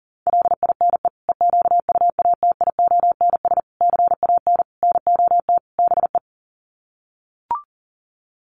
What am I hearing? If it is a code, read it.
Morse code, heard as a keyed tone, about 40 wpm, 700 Hz. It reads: LINE EQUATIONS CAN NOT BE
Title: Sets of 5 Words 40wpm